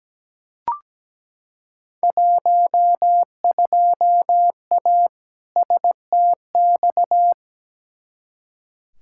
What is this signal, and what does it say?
Morse code, heard as a keyed tone, about 17 wpm, 700 Hz. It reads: 12A STX
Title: Field Day Exchanges Encore - 17wpm